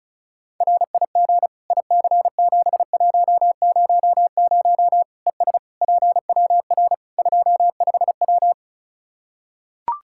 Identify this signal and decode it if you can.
Morse code, heard as a keyed tone, about 36 wpm, 700 Hz. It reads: RIG IC7100 ES PWR 25W
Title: K1USN SST QSO Practice 35wpm